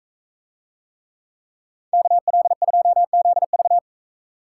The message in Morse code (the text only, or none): KL2ZV